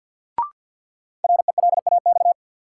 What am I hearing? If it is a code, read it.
Morse code, heard as a keyed tone, about 50 wpm, 700 Hz. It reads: RELAX
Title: Sentences from Top 1400 Words - New Word Drill Alt - 50wpm